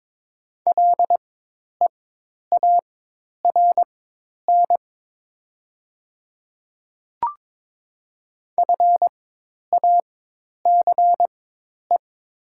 Morse code, heard as a keyed tone, about 22 wpm, spaced as for 10 wpm, 700 Hz. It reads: LEARN FACE
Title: 4x Character Spacing - Top 300 Words - 22wpm